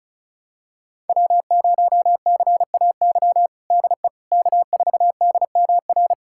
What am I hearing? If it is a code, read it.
Morse code, heard as a keyed tone, about 36 wpm, 700 Hz. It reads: W0CAY DE K4DMR
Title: USA Callsign DE Callsign - 5 Characters - 35wpm